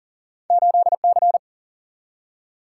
Morse code, heard as a keyed tone, about 39 wpm, 700 Hz. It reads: 8C
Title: Single-Letter Single-Number 40wpm